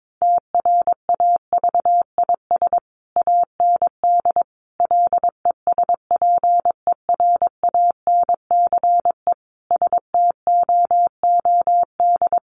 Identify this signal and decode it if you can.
Morse code, heard as a keyed tone, about 22 wpm, 700 Hz. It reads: TRAVIS AND LESPERANCE STOOD
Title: A Sound Of Thunder 22wpm